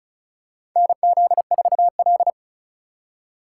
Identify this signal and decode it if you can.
Morse code, heard as a keyed tone, about 36 wpm, 700 Hz. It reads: NZ4L
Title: USA Call Signs - 4 Characters - 2x Character Spacing - 35wpm